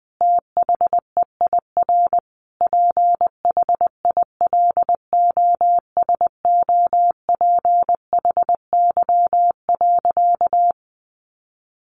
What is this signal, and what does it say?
Morse code, heard as a keyed tone, about 19 wpm, 700 Hz. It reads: THEIR PHILOSOPHY.
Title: Sentences from Top 900 Words 20wpm